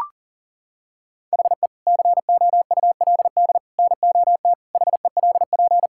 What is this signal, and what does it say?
Morse code, heard as a keyed tone, about 40 wpm, 700 Hz. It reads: HE COULD NOT HELP
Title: Sentences from Top 1600 Words 40wpm